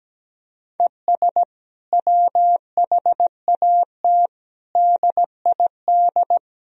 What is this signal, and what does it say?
Morse code, heard as a keyed tone, about 17 wpm, 700 Hz. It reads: ES WHAT DID
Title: Sentences from Top 300 Words and Abbreviations 17wpm